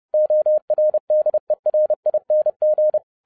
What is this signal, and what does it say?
Morse code, heard as a keyed tone, about 30 wpm, 610 Hz. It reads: ORDERING
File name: 2011-10-mp3 / n10-Czech Army Morse Keys 30wpm